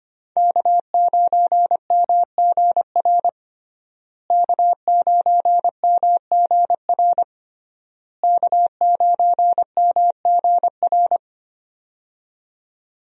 Morse code, heard as a keyed tone, about 25 wpm, 700 Hz. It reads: K9MGR K9MGR K9MGR
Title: USA Call Signs - 5 Characters - Repeated 3x - 25wpm